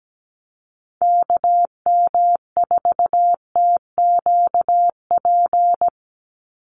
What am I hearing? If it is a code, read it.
Morse code, heard as a keyed tone, about 17 wpm, 700 Hz. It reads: KM4TQP
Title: USA Call Signs 17wpm